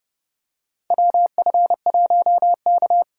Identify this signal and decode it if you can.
Morse code, heard as a keyed tone, about 30 wpm, 700 Hz. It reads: WF1K
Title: USA Call Signs - 4 Characters - 4x Character Spacing - 30wpm